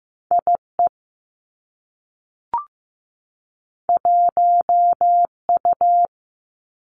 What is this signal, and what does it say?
Morse code, heard as a keyed tone, about 15 wpm, 700 Hz. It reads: I E 1 U
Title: Two Challenging Letter-Number Combinations 15wpm